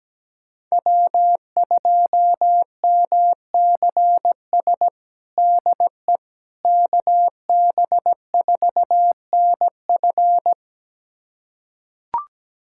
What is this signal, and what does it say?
Morse code, heard as a keyed tone, about 17 wpm, 700 Hz. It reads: W2MCS DE KB4NF
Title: USA Callsign DE Callsign - 5 Characters - 17wpm